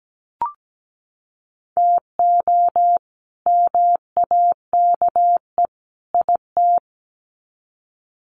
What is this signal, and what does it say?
Morse code, heard as a keyed tone, about 17 wpm, 700 Hz. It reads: TO MAKE IT
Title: Sets of 3 Words 17wpm